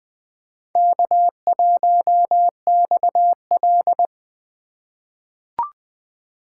Morse code, heard as a keyed tone, about 20 wpm, 700 Hz. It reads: K1XL
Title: USA Call Signs - 4 Characters - 3x Character Spacing - 20wpm